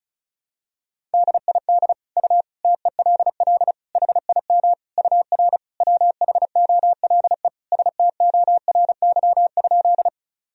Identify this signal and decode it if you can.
Morse code, heard as a keyed tone, about 34 wpm, 700 Hz. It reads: DID U TELL HIM UR WHOLE STORY?
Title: Sentences from Top 400 Words and Abbreviations 35wpm